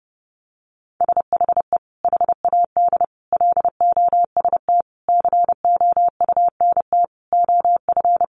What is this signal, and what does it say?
Morse code, heard as a keyed tone, about 28 wpm, 700 Hz. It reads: SHE HAD LOST COUNT OF
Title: Sentences from Top 1500 Words 30wpm